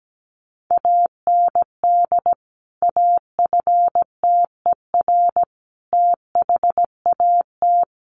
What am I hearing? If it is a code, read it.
Morse code, heard as a keyed tone, about 17 wpm, 700 Hz. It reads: AND AFTER THAT
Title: Sentences from Top 1400 Words 17wpm